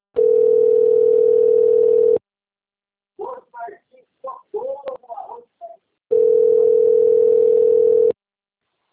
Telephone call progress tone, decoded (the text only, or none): ringback tone